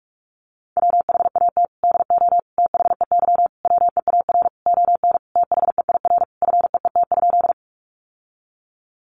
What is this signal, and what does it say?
Morse code, heard as a keyed tone, about 44 wpm, 700 Hz. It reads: WHAT DO THEY WEAR ON THEIR FEET?
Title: Sentences from Top 1500 Words 45wpm